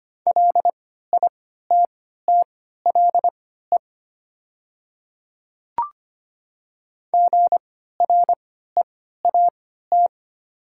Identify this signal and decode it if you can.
Morse code, heard as a keyed tone, about 25 wpm, spaced as for 14 wpm, 700 Hz. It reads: LITTLE GREAT
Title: 3x Character Spacing - Top 200 Words - 25wpm